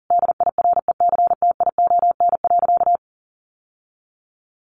Morse code, heard as a keyed tone, about 40 wpm, 700 Hz. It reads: DIRECTION.
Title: Sentences from Top 1300 Words 40wpm